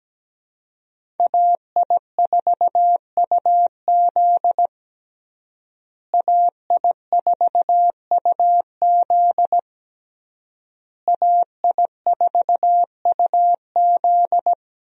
Morse code, heard as a keyed tone, about 17 wpm, 700 Hz. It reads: AI4UZ AI4UZ AI4UZ
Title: USA Call Signs - 5 Characters - Repeated 3x - 17wpm